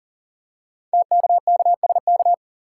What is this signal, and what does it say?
Morse code, heard as a keyed tone, about 42 wpm, 690 Hz. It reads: TKKSK
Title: Code Groups - 5 Letters - 40wpm